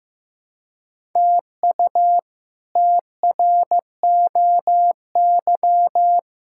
Morse code, heard as a keyed tone, about 15 wpm, 700 Hz. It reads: TU TROY
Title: K1USN SST QSO Practice - 15wpm